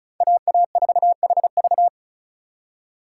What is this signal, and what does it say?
Morse code, heard as a keyed tone, about 34 wpm, 700 Hz. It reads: AA4HV
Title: K1USN SST QSO Practice - 35wpm